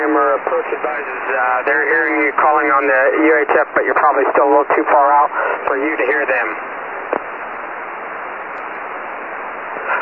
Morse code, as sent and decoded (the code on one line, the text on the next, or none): none